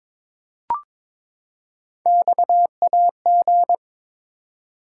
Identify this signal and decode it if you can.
Morse code, heard as a keyed tone, about 23 wpm, 700 Hz. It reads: XAG
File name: practice / 22wpm / Three Letter-Number Combinations - 22wpm